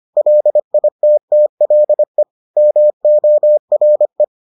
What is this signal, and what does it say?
Morse code, heard as a keyed tone, about 25 wpm, 600 Hz. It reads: LITTLE MORE